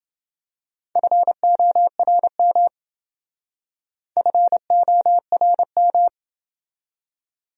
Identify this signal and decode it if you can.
Morse code, heard as a keyed tone, about 27 wpm, 700 Hz. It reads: FORM FORM